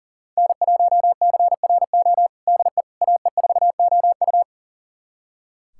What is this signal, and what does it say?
Morse code, heard as a keyed tone, about 40 wpm, 700 Hz. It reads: N1CRO DE AE4OU
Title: USA Callsign DE Callsign - 5 Characters - 40wpm